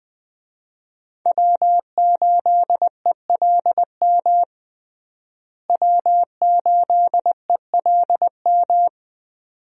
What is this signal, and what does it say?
Morse code, heard as a keyed tone, about 20 wpm, 700 Hz. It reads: W8ELM W8ELM
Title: USA Call Signs - 5 Characters - Repeated 2x - 20wpm